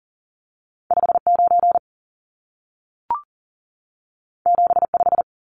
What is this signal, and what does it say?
Morse code, heard as a keyed tone, about 40 wpm, 700 Hz. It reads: 59 75
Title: Numbers - 2-Digit 40wpm